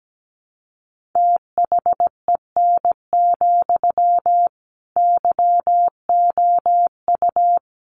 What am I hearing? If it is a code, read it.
Morse code, heard as a keyed tone, about 17 wpm, 700 Hz. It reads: THEN, YOU
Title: Sentences from Top 1500 Words 17wpm